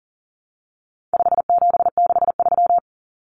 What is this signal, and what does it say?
Morse code, heard as a keyed tone, about 40 wpm, 700 Hz. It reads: 5763
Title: Common Cut Numbers - 4-Digit 40wpm